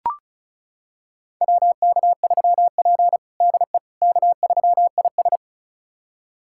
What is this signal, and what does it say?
Morse code, heard as a keyed tone, about 36 wpm, 700 Hz. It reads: WK3P DE K3IS
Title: USA Callsign DE Callsign - 4 Characters - 35wpm